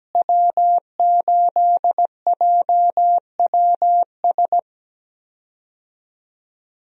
Morse code, heard as a keyed tone, about 17 wpm, 700 Hz. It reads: W8JWS
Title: USA Call Signs - 5 Characters - Repeated 2x - 17wpm